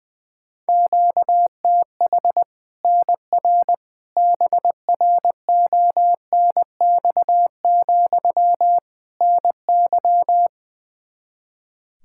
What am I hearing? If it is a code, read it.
Morse code, heard as a keyed tone, about 20 wpm, 700 Hz. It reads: QTH NR BRONX, NY
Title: Practice QSO - Exchange 1 - Beginning - 20wpm